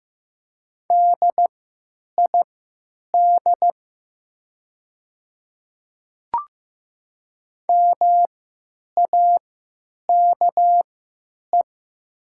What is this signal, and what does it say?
Morse code, heard as a keyed tone, about 15 wpm, spaced as for 9 wpm, 700 Hz. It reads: DID MAKE
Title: 3x Character Spacing - Top 200 Words - 15wpm